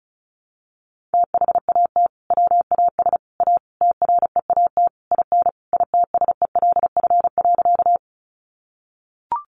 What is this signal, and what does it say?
Morse code, heard as a keyed tone, about 34 wpm, 700 Hz. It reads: THAT WAS A TREAT IN ITSELF.
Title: Sentences from Top 1300 Words 35wpm